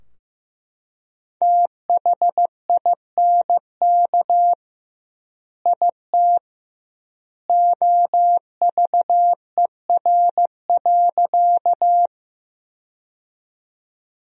Morse code, heard as a keyed tone, about 15 wpm, 700 Hz. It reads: THINK IT OVER.
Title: 2x Word-Spacing - Sentences from Top 200 Words - 15wpm